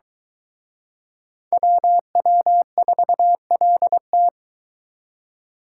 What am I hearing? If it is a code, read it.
Morse code, heard as a keyed tone, about 23 wpm, 700 Hz. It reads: WW4LT